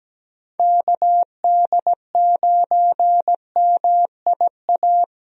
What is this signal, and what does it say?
Morse code, heard as a keyed tone, about 17 wpm, 700 Hz. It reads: KD9MIA
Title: USA Call Signs - 6 Characters - 17wpm